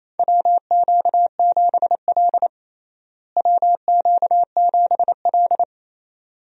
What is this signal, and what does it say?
Morse code, heard as a keyed tone, about 28 wpm, 700 Hz. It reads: WQ7L WQ7L
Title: USA Call Signs - 4 Characters - Repeated 2x - 28wpm